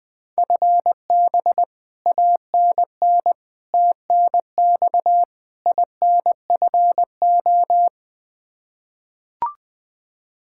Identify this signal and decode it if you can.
Morse code, heard as a keyed tone, about 21 wpm, 700 Hz. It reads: FB ANN TNX INFO
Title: Practice QSO - Exchange 3 - More Info - 20wpm